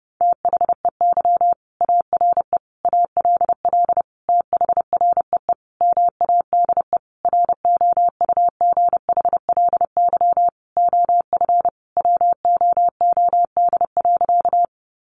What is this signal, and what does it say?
Morse code, heard as a keyed tone, about 30 wpm, 700 Hz. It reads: THEY ARE ALL THREE MADE ROUGHLY OF WOOD.
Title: Sentences from Top 1500 Words 30wpm